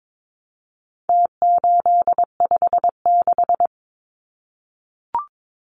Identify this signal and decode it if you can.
Morse code, heard as a keyed tone, about 22 wpm, 700 Hz. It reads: T856
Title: Numbers - Contest Style - 4-Digit 22wpm